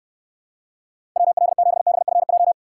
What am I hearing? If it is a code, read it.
Morse code, heard as a keyed tone, about 69 wpm, 700 Hz. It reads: WL7BFJ